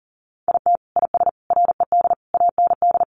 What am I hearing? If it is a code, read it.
Morse code, heard as a keyed tone, about 40 wpm, 700 Hz. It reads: IT IS RED AND
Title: Easy Phrases 40wpm